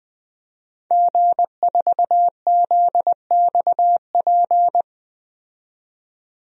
Morse code, heard as a keyed tone, about 20 wpm, 700 Hz. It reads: G4ZXP
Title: DXCC-100 Call Signs - 5 Characters - 20wpm